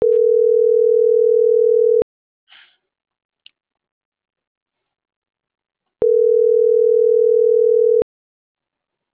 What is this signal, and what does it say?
Ringback tone (North America), 440 + 480 Hz.